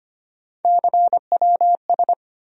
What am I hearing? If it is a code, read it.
Morse code, heard as a keyed tone, about 25 wpm, 700 Hz. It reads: CWS
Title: USA Call Sign Postfixes - 3 Letters 25wpm